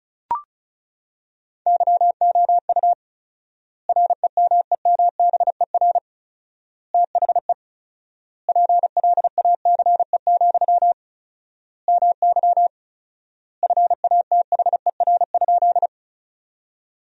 Morse code, heard as a keyed tone, about 36 wpm, 700 Hz. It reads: YOU REMEMBER THE PLACE, MY FATHER?
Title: 4x Word-Spacing - Sentences from Top 400 Words - 35wpm